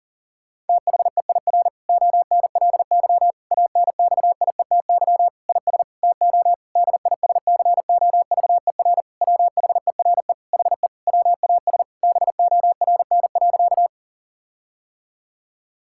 Morse code, heard as a keyed tone, about 37 wpm, 700 Hz. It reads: THEIR ONLY ANXIETY IS TO DISCOVER WHERE HE WAS BORN.
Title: Sentences from Top 1600 Words 40wpm